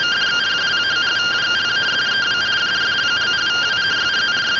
Two-tone teleprinter signal, about 170 Hz shift.